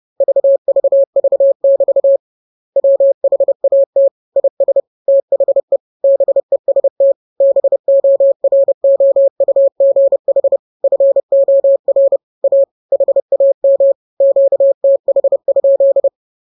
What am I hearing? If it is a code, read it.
Morse code, heard as a keyed tone, about 30 wpm, 550 Hz. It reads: VVV<BT> WHAT IS THE BEST BOROUGH FOR A HAM QTH?
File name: jokes / joke 15 30 0000